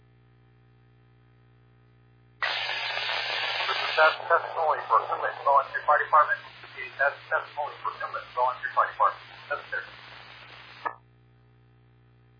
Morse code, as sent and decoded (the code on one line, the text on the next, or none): none